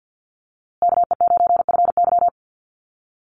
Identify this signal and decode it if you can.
Morse code, heard as a keyed tone, about 48 wpm, 700 Hz. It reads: KE9FY
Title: USA Call Signs Repeated 2x 50wpm